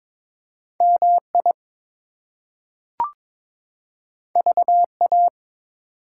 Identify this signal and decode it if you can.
Morse code, heard as a keyed tone, about 22 wpm, 700 Hz. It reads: MI VA
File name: ARRL Sections v2 - 22wpm